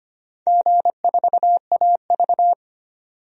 Morse code, heard as a keyed tone, about 25 wpm, 700 Hz. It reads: G4AV
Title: DXCC-100 Call Signs - 4 Characters - 25wpm